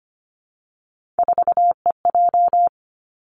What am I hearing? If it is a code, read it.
Morse code, heard as a keyed tone, about 25 wpm, 700 Hz. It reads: 4EJ
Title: Three Challenging Letter-Number Combinations 25wpm